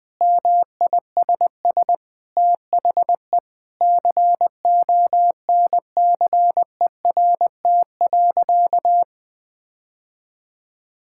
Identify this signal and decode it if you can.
Morse code, heard as a keyed tone, about 20 wpm, 700 Hz. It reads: MISS THE CONCERT.
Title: Sentences with 8-Word Phrases - Spoken Beginning - 20wpm